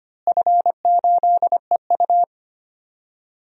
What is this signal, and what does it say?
Morse code, heard as a keyed tone, about 25 wpm, 700 Hz. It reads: F8EU
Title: DXCC-100 Call Signs - 4 Characters - Repeated 2x - 25wpm